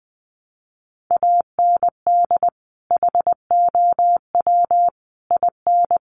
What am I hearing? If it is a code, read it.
Morse code, heard as a keyed tone, about 20 wpm, 700 Hz. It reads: AND HOW IN
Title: Sentences from Top 200 Words 20wpm